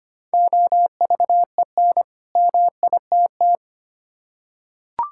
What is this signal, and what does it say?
Morse code, heard as a keyed tone, about 25 wpm, 690 Hz. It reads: OVEN MITT
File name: practice / 25wpm / Easy Words with Hints - 25wpm